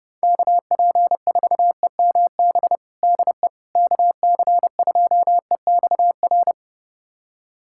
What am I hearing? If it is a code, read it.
Morse code, heard as a keyed tone, about 30 wpm, 700 Hz. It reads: KP4EMB DE KC2EXR
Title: USA Callsign DE Callsign 30wpm